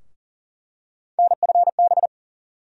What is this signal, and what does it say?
Morse code, heard as a keyed tone, about 37 wpm, 700 Hz. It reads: NFB
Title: USA Call Sign Postfixes - 3 Letters 40wpm